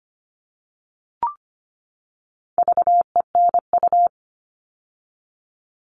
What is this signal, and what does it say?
Morse code, heard as a keyed tone, about 25 wpm, 700 Hz. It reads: VENU
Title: All Cut Numbers - 4-Digit 25wpm